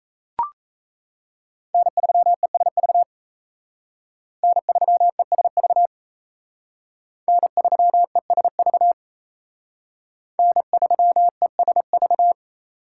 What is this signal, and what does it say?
Morse code, heard as a keyed tone, about 34 wpm, 700 Hz. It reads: N3ESV N3ESV N3ESV N3ESV
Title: Speed Racing - USA 1x3 Call Signs 28wpm